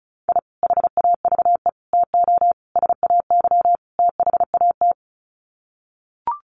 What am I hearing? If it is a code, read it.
Morse code, heard as a keyed tone, about 34 wpm, 700 Hz. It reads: I HAVE TO SAY THAT
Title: Sets of 5 Words Encore 35wpm